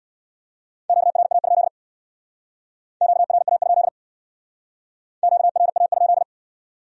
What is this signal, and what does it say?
Morse code, heard as a keyed tone, about 62 wpm, 700 Hz. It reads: QRA? QRA? QRA?
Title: Speed-Racing Q-Codes - 50wpm